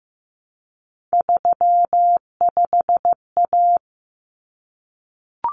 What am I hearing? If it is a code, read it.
Morse code, heard as a keyed tone, about 15 wpm, 700 Hz. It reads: 35A
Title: Common Cut Numbers - 3-Digit 15wpm